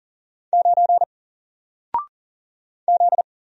Morse code, heard as a keyed tone, about 40 wpm, 700 Hz. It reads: 9 Z
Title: Single Letter-Number Alt 40wpm